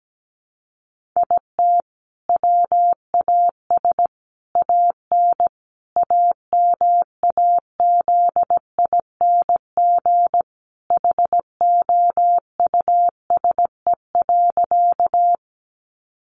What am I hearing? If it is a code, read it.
Morse code, heard as a keyed tone, about 17 wpm, 700 Hz. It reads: IT WAS AN AMAZING HOUSE.
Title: Sentences from Top 1500 Words 17wpm